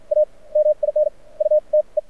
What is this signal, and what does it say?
Morse code, heard as a keyed tone, about 53 wpm, spaced as for 35 wpm, 600 Hz. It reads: A MINUTE
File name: archived / CRSnet